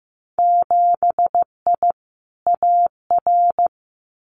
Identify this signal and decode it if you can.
Morse code, heard as a keyed tone, about 15 wpm, 700 Hz. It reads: 7I AR
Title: Winter Field Day Exchanges Encore 15wpm